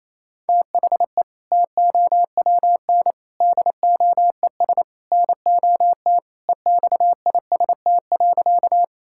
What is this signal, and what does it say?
Morse code, heard as a keyed tone, about 28 wpm, 700 Hz. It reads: THE TOWN DOES NOT EXIST.
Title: Sentences from Top 1000 Words - Review Part 5 - 28wpm